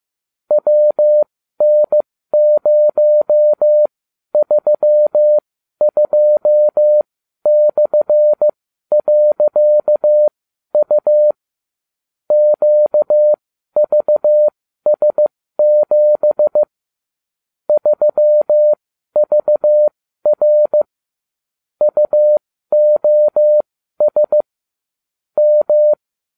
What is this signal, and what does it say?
Morse code, heard as a keyed tone, about 15 wpm, spaced as for 12 wpm, 600 Hz. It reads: WN032/.U QVS7 3VR UOS M